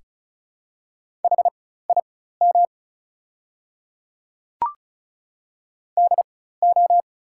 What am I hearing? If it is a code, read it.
Morse code, heard as a keyed tone, about 34 wpm, spaced as for 16 wpm, 700 Hz. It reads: HIM DO
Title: 4x Character Spacing - Top 100 Words - 35wpm